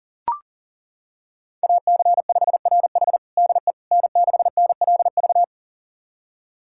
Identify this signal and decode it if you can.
Morse code, heard as a keyed tone, about 40 wpm, 700 Hz. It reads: AC5RH DE N6NLV
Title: USA Callsign DE Callsign 40wpm